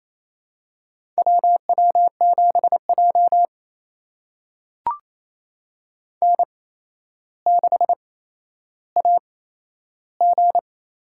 Morse code, heard as a keyed tone, about 28 wpm, spaced as for 12 wpm, 700 Hz. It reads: WW7J N6AG
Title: USA Call Signs - 4 Characters - 8x Character Spacing - 28wpm